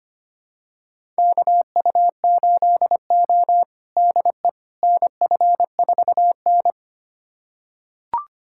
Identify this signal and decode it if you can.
Morse code, heard as a keyed tone, about 25 wpm, 700 Hz. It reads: KU8O DE NF4N
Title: USA Callsign DE Callsign - 4 Characters - 25wpm